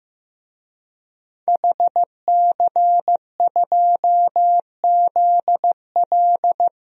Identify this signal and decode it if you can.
Morse code, heard as a keyed tone, about 15 wpm, 700 Hz. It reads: HC2ZL
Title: DXCC-100 Call Signs - 5 Characters - Repeated 2x - 15wpm